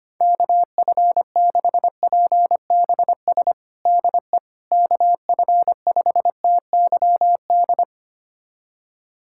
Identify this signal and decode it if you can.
Morse code, heard as a keyed tone, about 25 wpm, 700 Hz. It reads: KF6PBS DE KF5TYD
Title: USA Callsign DE Callsign - 6 Characters - 25wpm